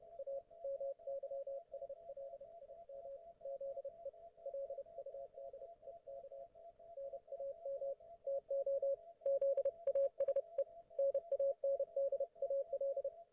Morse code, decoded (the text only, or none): A MYSLIM ZE LUDEK NAM TO ZASE NANDAL